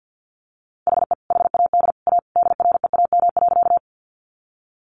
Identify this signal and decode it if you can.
Morse code, heard as a keyed tone, about 50 wpm, 700 Hz. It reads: HE HAD A DREAM.
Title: Sentences from Top 1500 Words 50wpm